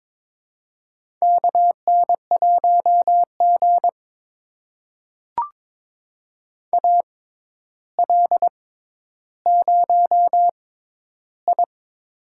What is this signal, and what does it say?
Morse code, heard as a keyed tone, about 22 wpm, spaced as for 11 wpm, 700 Hz. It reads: KN1G AL0I
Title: USA Call Signs - 4 Characters - 6x Character Spacing - 22wpm